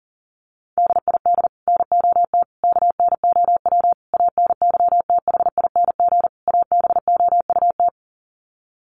Morse code, heard as a keyed tone, about 40 wpm, 700 Hz. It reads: DID NOT KNOW ANYTHING ABOUT
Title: Sets of 5 Words Encore2 40wpm